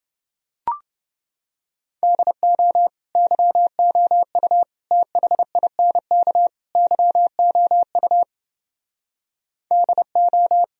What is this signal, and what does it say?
Morse code, heard as a keyed tone, about 28 wpm, 700 Hz. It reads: DO YOU THINK YOU DO